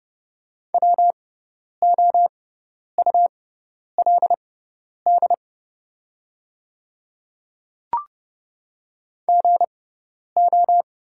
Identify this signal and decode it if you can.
Morse code, heard as a keyed tone, about 30 wpm, spaced as for 10 wpm, 700 Hz. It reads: WOULD GO
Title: 6x Character Spacing - Top 100 Words - 30wpm